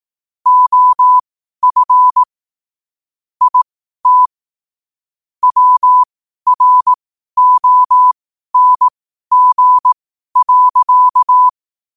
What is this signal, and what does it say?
Morse code, heard as a keyed tone, about 18 wpm, spaced as for 12 wpm, 1000 Hz. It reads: OF IT WRONG.
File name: qotdfiles / 12wpm